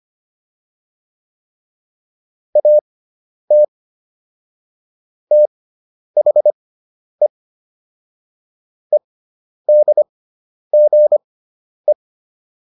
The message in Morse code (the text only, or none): AT THE EDGE